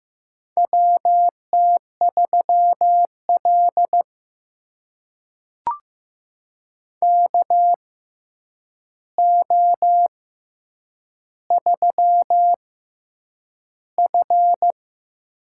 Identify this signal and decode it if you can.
Morse code, heard as a keyed tone, about 15 wpm, spaced as for 8 wpm, 700 Hz. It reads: WT3L KO3F
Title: USA Call Signs - 4 Characters - 6x Character Spacing - 15wpm